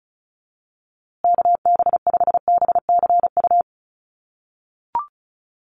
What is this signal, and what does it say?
Morse code, heard as a keyed tone, about 34 wpm, 700 Hz. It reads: KB5BCU
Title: USA Call Signs Repeated 2x 35wpm